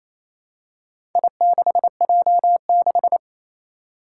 Morse code, heard as a keyed tone, about 27 wpm, 700 Hz. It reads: I6J6
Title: Four Challenging Letter-Number Combinations 28wpm